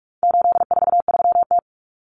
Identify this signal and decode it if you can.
Morse code, heard as a keyed tone, about 44 wpm, 700 Hz. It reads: 843T
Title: Common Cut Numbers - 4-Digit 45wpm